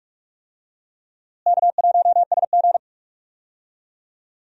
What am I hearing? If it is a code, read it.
Morse code, heard as a keyed tone, about 46 wpm, 700 Hz. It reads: K1SG
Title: USA Call Signs - 4 Characters - Repeated 2x - 45wpm